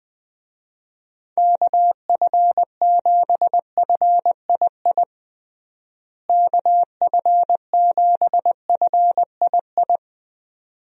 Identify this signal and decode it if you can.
Morse code, heard as a keyed tone, about 20 wpm, 700 Hz. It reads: KF7FII KF7FII
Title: USA Call Signs - 6 Characters - Repeated 2x - 20wpm